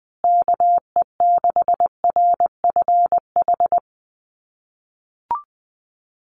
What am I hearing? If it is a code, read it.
Morse code, heard as a keyed tone, about 20 wpm, 700 Hz. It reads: KE6RFH
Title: USA Call Signs Repeated 3x 20wpm